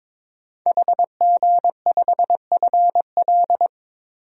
Morse code, heard as a keyed tone, about 22 wpm, 700 Hz. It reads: HG5FL
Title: DXCC-100 Call Signs - 5 Characters - 22wpm